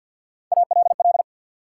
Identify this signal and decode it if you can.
Morse code, heard as a keyed tone, about 53 wpm, 700 Hz. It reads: ALL
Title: Sentences from Top 100 Words - New Word Drill - 50wpm